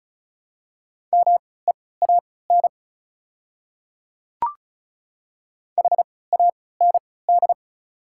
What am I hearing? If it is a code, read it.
Morse code, heard as a keyed tone, about 34 wpm, spaced as for 20 wpm, 690 Hz. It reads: MEAN HAND
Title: 3x Character Spacing - Top 200 Words - 35wpm